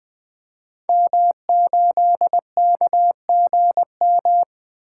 Morse code, heard as a keyed tone, about 20 wpm, 700 Hz. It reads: M8KGM
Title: Code Groups - 5 Letter:Numbers - 20wpm